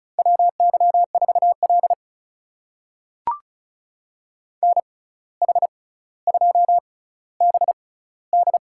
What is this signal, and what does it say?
Morse code, heard as a keyed tone, about 36 wpm, spaced as for 17 wpm, 700 Hz. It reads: WY4L NH2BD
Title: USA Call Signs - 6x Character Spacing - 35wpm